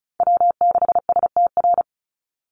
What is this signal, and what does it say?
Morse code, heard as a keyed tone, about 34 wpm, 700 Hz. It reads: W6STR
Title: NA SOTA Chaser Callsigns 35wpm